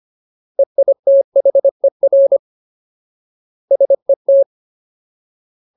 Morse code, heard as a keyed tone, about 25 wpm, 550 Hz. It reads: EITHER SET
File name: Ultimate Word List / 25WPM Ultimate